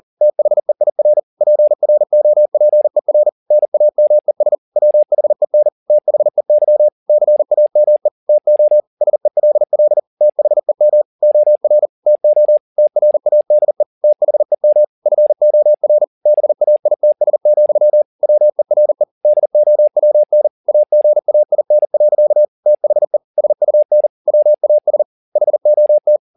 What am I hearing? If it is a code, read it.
Morse code, heard as a keyed tone, about 40 wpm, 600 Hz. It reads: THEIR PROPER NAMES WHEN THEY CAME TO SELL THEM OR TO TRADE THEM FOR BAITS, WERE DOWN AGAIN. THE SUN WAS HOT